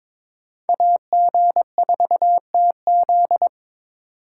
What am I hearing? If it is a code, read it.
Morse code, heard as a keyed tone, about 25 wpm, 700 Hz. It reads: AG4TZ
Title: Speed Racing - USA 2x2 Call Signs 22wpm